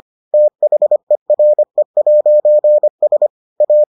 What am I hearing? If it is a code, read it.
Morse code, heard as a keyed tone, about 25 wpm, 600 Hz. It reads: THERE'S A